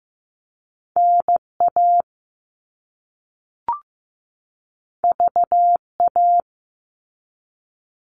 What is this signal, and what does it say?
Morse code, heard as a keyed tone, about 15 wpm, 700 Hz. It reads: NA VA
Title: All Cut Numbers - 2-Digit 15wpm